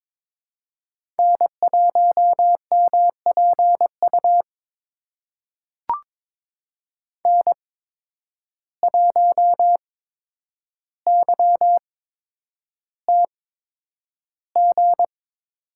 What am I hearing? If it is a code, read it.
Morse code, heard as a keyed tone, about 22 wpm, spaced as for 9 wpm, 700 Hz. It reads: N1MPU N1YTG
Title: USA Call Signs - 5 Characters - 8x Character Spacing - 22wpm